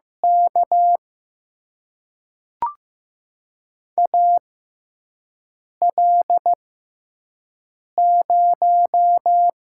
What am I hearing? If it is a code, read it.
Morse code, heard as a keyed tone, about 15 wpm, spaced as for 5 wpm, 700 Hz. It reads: K AL0
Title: USA Call Signs - 4 Characters - 6x Character Spacing - 15wpm